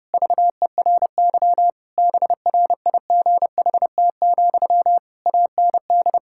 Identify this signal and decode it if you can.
Morse code, heard as a keyed tone, about 32 wpm, 700 Hz. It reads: VERY BRIGHT, AND
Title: Sentences from Top 1600 Words 30wpm